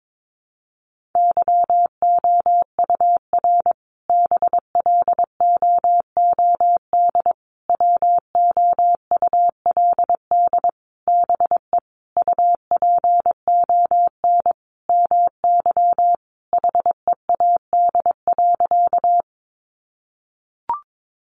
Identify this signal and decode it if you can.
Morse code, heard as a keyed tone, about 22 wpm, 700 Hz. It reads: YOUR BLOOD WOULD BE UPON MY HEAD.
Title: Sentences from Top 1100 Words 22wpm